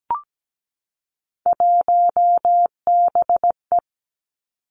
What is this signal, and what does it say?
Morse code, heard as a keyed tone, about 17 wpm, 700 Hz. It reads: 1BE
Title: Three Challenging Letter-Number Combinations 17wpm